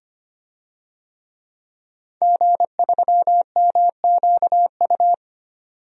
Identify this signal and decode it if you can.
Morse code, heard as a keyed tone, about 25 wpm, 700 Hz. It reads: G3MQU
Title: DXCC-100 Call Signs - 5 Characters - 25wpm